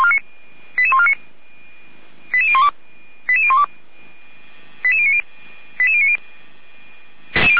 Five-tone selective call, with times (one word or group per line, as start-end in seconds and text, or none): 0.8-1.1 88149
2.3-2.7 88012
3.3-3.6 88012
4.8-5.2 88090
5.8-6.2 88090